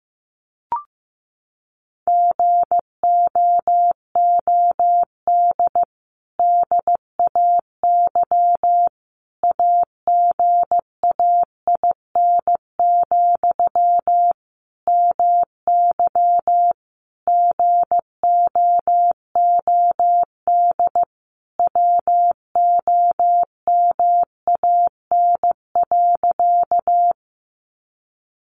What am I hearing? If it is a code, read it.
Morse code, heard as a keyed tone, about 15 wpm, 700 Hz. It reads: GOOD DAY AGAIN, MY GOOD WOMAN.
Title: Sentences from Top 1100 Words 15wpm